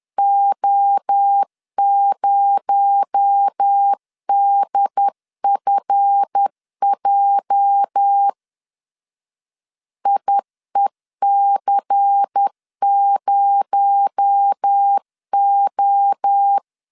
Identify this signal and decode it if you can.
Morse code, heard as a keyed tone, about 10 wpm, 800 Hz. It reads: O0DFJ IEC0O
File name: LEZIONI / 8 YJ45 / 8 YJ45 10wpm 2549